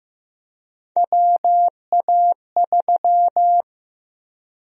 Morse code, heard as a keyed tone, about 16 wpm, 700 Hz. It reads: WA3
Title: USA Call Sign Prefixes - 15wpm